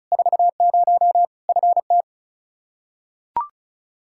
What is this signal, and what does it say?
Morse code, heard as a keyed tone, about 36 wpm, 700 Hz. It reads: 40 FT
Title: Practice QSO - Exchange 2 - Info - 35wpm